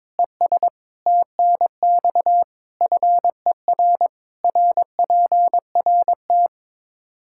Morse code, heard as a keyed tone, about 22 wpm, 700 Hz. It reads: ES TNX FER RPRT
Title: Practice QSO - Exchange 2 - Info - 22wpm